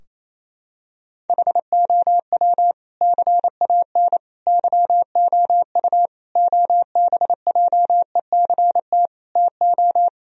Morse code, heard as a keyed tone, about 28 wpm, 700 Hz. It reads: HOW CAN YOU OBJECT TO
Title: Sentences from Top 1000 Words - Review Part 3 - 28wpm